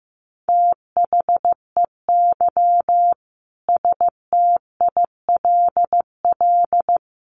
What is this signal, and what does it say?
Morse code, heard as a keyed tone, about 15 wpm, 700 Hz. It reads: THEY STILL
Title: Sentences from Top 1200 Words 15wpm